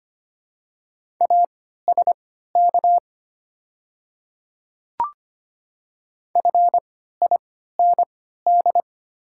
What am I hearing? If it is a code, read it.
Morse code, heard as a keyed tone, about 25 wpm, spaced as for 14 wpm, 700 Hz. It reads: ASK FIND
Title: 3x Character Spacing - Top 200 Words - 25wpm